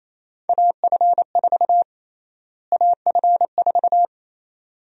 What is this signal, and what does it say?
Morse code, heard as a keyed tone, about 28 wpm, 700 Hz. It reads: AF4 AF4
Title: USA Call Sign Prefixes - Repeated 2x - 28wpm